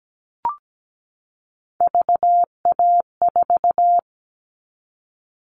Morse code, heard as a keyed tone, about 17 wpm, 700 Hz. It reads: VA4
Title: All Cut Numbers - 3-Digit 17wpm